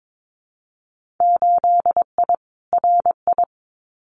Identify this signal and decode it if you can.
Morse code, heard as a keyed tone, about 22 wpm, 700 Hz. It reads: 8I RI
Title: Winter Field Day Exchanges Encore 22wpm